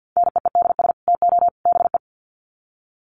Morse code, heard as a keyed tone, about 50 wpm, 700 Hz. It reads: NEEDS TO BE
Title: Sets of 3 Words 50wpm